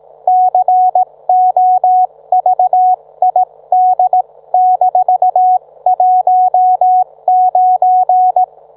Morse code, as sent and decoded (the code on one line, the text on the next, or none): -.-. --- ...- .. -.. -....- .---- ----.
COVID-19